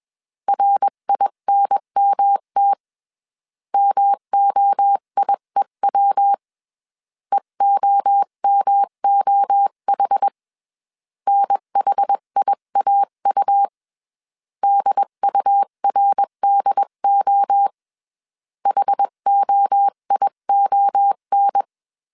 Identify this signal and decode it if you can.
Morse code, heard as a keyed tone, about 21 wpm, 800 Hz. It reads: RINMT MOIEW EOMOH NHIAU DURDO HOION
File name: 20wpm